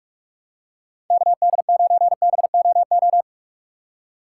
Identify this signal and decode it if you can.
Morse code, heard as a keyed tone, about 46 wpm, 700 Hz. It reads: KD9BOO
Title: USA Call Signs - 6 Characters - 3x Character Spacing - 45wpm